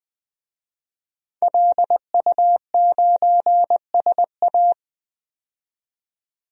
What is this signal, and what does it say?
Morse code, heard as a keyed tone, about 20 wpm, 700 Hz. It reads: LU9SA
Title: DXCC-100 Call Signs - 5 Characters - 20wpm